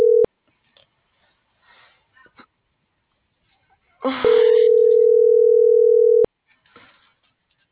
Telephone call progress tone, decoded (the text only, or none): ringback tone